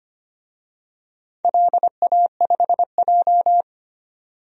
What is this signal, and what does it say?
Morse code, heard as a keyed tone, about 25 wpm, 700 Hz. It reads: LA5J
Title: DXCC-100 Call Signs - 4 Characters - 25wpm